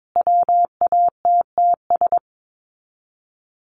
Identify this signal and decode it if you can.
Morse code, heard as a keyed tone, about 22 wpm, 700 Hz. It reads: WATTS
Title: Common QSO Phrases 22wpm